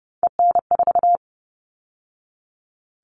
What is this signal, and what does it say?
Morse code, heard as a keyed tone, about 30 wpm, 700 Hz. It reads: EN4
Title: All Cut Numbers - 3-Digit 30wpm